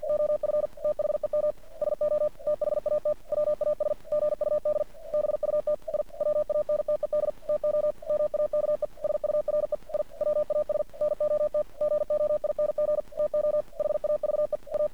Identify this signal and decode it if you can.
Morse code, heard as a keyed tone, about 48 wpm, 600 Hz. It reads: OF THEM SO THAT WAS GUD BUT I WANTED TO MAKE SURE I WAS NOT GOING TO HAVE A
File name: archived / CRSnet